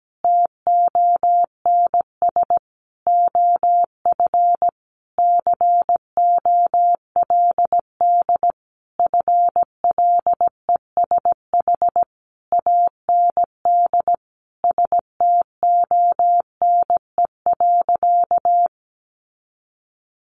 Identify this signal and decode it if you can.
Morse code, heard as a keyed tone, about 17 wpm, 700 Hz. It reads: TONS OF COLD FLESH AND STONE.
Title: A Sound Of Thunder 17wpm